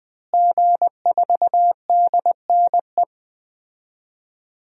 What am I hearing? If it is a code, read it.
Morse code, heard as a keyed tone, about 20 wpm, 700 Hz. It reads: G4DNE
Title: DXCC-100 Call Signs - 5 Characters - Repeated 3x - 20wpm